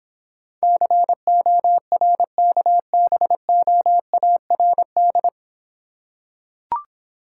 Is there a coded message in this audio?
Yes: Morse code from 0.6 to 5.3 s, heard as a keyed tone, about 25 wpm, 700 Hz. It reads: CORKBOARD